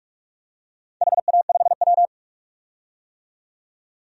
Morse code, heard as a keyed tone, about 47 wpm, 700 Hz. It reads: HA5W
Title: DXCC-100 Call Signs - 4 Characters - 45wpm